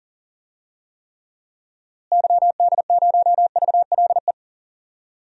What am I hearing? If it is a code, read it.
Morse code, heard as a keyed tone, about 40 wpm, 700 Hz. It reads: YD0VLE
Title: DXCC-100 Call Signs - 6 Characters - Repeated 3x - 40wpm